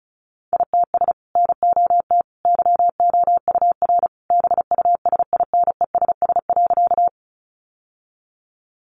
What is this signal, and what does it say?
Morse code, heard as a keyed tone, about 34 wpm, 700 Hz. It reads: ITS NOT YOUR BUSINESS.
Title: Sentences from Top 600 Words 35wpm